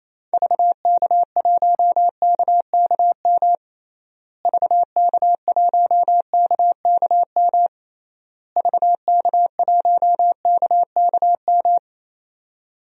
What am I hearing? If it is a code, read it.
Morse code, heard as a keyed tone, about 28 wpm, 700 Hz. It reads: VK1KKM VK1KKM VK1KKM
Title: DXCC-100 Call Signs - 6 Characters - Repeated 3x - 28wpm